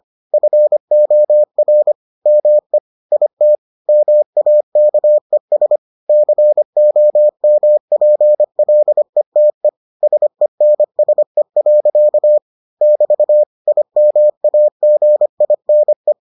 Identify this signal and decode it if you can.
Morse code, heard as a keyed tone, about 25 wpm, 600 Hz. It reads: FOR ME IT MAKES COMPLETE SENSE. <BT> IMAGINE